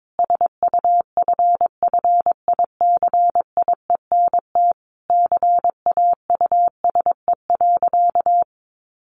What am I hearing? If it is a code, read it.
Morse code, heard as a keyed tone, about 22 wpm, 700 Hz. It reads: SUFFICIENT CAUSE.
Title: Sentences from Top 1400 Words 22wpm